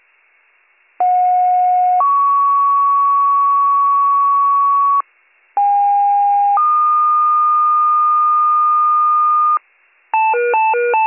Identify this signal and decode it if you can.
Two-tone page, heard as a one-second tone, then a three-second tone, about 730 then 1120 Hz, about 810 then 1220 Hz. It is heard from 1.0 to 9.6 s.